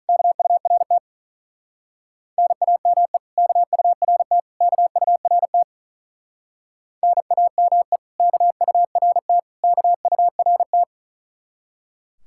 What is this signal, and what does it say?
Morse code, heard as a keyed tone, about 45 wpm, 700 Hz. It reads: KURT NAME KURT KURT NAME KURT KURT